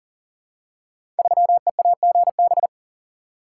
Morse code, heard as a keyed tone, about 40 wpm, 700 Hz. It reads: 3EAGB
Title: Code Groups - 5 Letter:Numbers - 40wpm